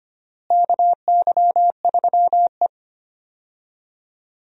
Morse code, heard as a keyed tone, about 25 wpm, 700 Hz. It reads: KY3E
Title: USA Call Signs - 4 Characters - Repeated 2x - 25wpm